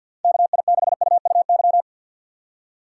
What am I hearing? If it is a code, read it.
Morse code, heard as a keyed tone, about 47 wpm, 700 Hz. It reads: KI6UUY
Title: USA Call Signs - 6 Characters - 4x Character Spacing - 50wpm